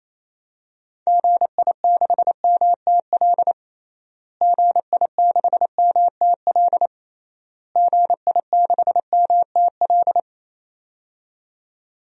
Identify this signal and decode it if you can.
Morse code, heard as a keyed tone, about 28 wpm, 700 Hz. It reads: GI6MTL GI6MTL GI6MTL
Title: DXCC-100 Call Signs - 6 Characters - Repeated 3x - 28wpm